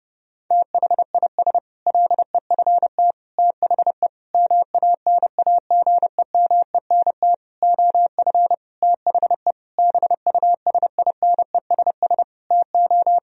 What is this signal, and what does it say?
Morse code, heard as a keyed tone, about 30 wpm, 700 Hz. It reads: THIS LEFT THE MANAGEMENT OF THE BUSINESS TO
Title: Sentences from Top 1000 Words - Review Part 4 - 30wpm